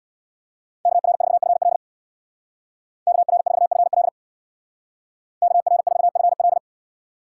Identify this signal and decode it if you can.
Morse code, heard as a keyed tone, about 62 wpm, 700 Hz. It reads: KR4FL KR4FL KR4FL
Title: Speed Racing - USA 2x2 Call Signs 50wpm